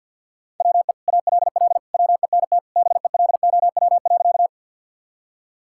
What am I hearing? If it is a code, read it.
Morse code, heard as a keyed tone, about 53 wpm, 700 Hz. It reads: WE ALL WENT BELOW.
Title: Sentences from Top 1000 Words - Review Part 5 - 50wpm